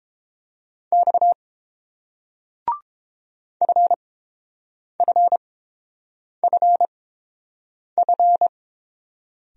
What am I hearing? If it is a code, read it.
Morse code, heard as a keyed tone, about 29 wpm, spaced as for 7 wpm, 700 Hz. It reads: X FFFF